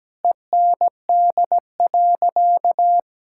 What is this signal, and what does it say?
Morse code, heard as a keyed tone, about 17 wpm, 700 Hz. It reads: END.
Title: Sentences from Top 1000 Words - Review Part 6 - 17wpm